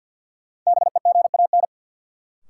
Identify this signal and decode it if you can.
Morse code, heard as a keyed tone, about 52 wpm, 700 Hz. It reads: BEGAN
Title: Sentences from Top 300 Words - New Word Drill - 50wpm